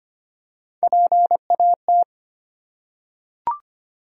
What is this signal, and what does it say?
Morse code, heard as a keyed tone, about 25 wpm, 700 Hz. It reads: PAT
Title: USA Common Names - Max 5 Letters - 25wpm